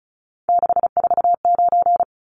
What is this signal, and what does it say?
Morse code, heard as a keyed tone, about 34 wpm, 700 Hz. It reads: 649
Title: Numbers - 3-Digit 35wpm